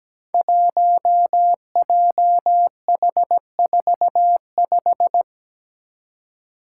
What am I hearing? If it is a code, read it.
Morse code, heard as a keyed tone, about 17 wpm, 700 Hz. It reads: 1JH45
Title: Five Challenging Letter-Number Combinations 17wpm